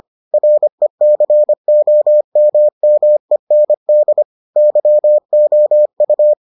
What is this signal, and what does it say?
Morse code, heard as a keyed tone, about 25 wpm, 600 Hz. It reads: RECOMMEND YOU